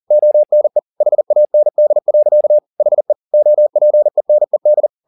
Morse code, heard as a keyed tone, about 40 wpm, 600 Hz. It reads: ONE HAND. HE OPENED